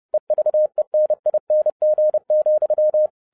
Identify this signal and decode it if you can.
Morse code, heard as a keyed tone, about 30 wpm, 610 Hz. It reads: EVENING,
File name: 2012-01-mp3 / n12-I DIDNT MISS THE SOLAR CYCLE 24-30wpm